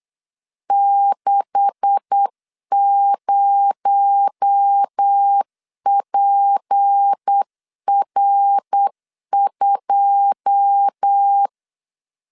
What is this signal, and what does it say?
Morse code, heard as a keyed tone, about 8 wpm, 800 Hz. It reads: 60PR2
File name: LEZIONI / 9 286 8wpm 3425